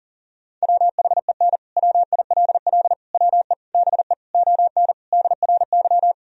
Morse code, heard as a keyed tone, about 40 wpm, 700 Hz. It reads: WHEN WILL WE BE ON DRY
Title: Sentences from Top 1000 Words - Review Part 3 - 40wpm